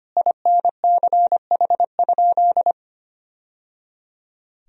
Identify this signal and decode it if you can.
Morse code, heard as a keyed tone, about 25 wpm, 700 Hz. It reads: INCH?
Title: Sentences from Top 1000 Words - Review Part 3 - 25wpm